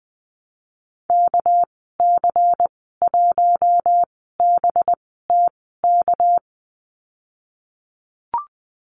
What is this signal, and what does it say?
Morse code, heard as a keyed tone, about 20 wpm, spaced as for 14 wpm, 700 Hz. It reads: KC1BTK
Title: 200 Lesson 32g - Letter k - Call Signs 20wpm